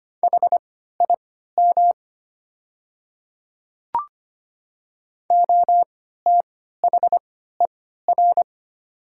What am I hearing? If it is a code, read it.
Morse code, heard as a keyed tone, about 25 wpm, spaced as for 14 wpm, 700 Hz. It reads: HIM OTHER
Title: 3x Character Spacing - Top 200 Words - 25wpm